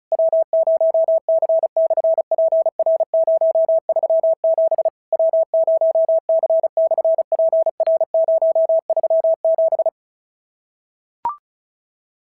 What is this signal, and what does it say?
Morse code, heard as a keyed tone, about 36 wpm, 690 Hz. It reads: W0C/PR037 W0C/PR037
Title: SOTA QSO Practice 35wpm